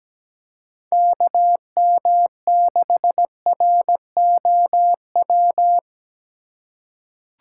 Morse code, heard as a keyed tone, about 17 wpm, 700 Hz. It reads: KM6ROW
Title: USA Call Signs - 6 Characters - 3x Character Spacing - 17wpm